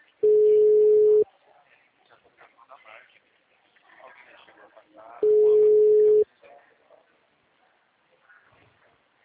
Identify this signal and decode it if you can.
Ringback tone (Europe, 425 Hz), with speech.